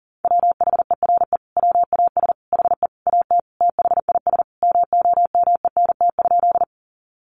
Morse code, heard as a keyed tone, about 40 wpm, 700 Hz. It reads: WHERE WAS HE AT THIS MOMENT?
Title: Sentences from Top 1000 Words 40wpm